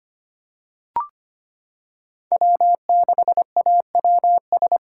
Morse code, heard as a keyed tone, about 25 wpm, 700 Hz. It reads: W6AWS
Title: K1USN SST QSO Practice - 25wpm